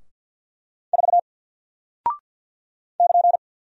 Morse code, heard as a keyed tone, about 50 wpm, 700 Hz. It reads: 4 <KN>
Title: Single Character (Letter:Number:Punctuation:Prosign) 50wpm